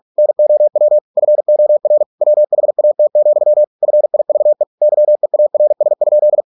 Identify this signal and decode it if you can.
Morse code, heard as a keyed tone, about 46 wpm, 600 Hz. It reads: NOW FOR WHAT, FIVE YEARS?